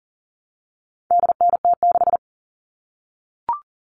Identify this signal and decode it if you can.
Morse code, heard as a keyed tone, about 38 wpm, 700 Hz. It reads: DNT6
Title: All Cut Numbers - 4-Digit 40wpm